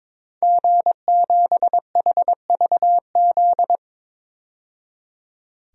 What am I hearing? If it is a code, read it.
Morse code, heard as a keyed tone, about 22 wpm, 700 Hz. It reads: G7HVZ
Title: DXCC-100 Call Signs - 5 Characters - 22wpm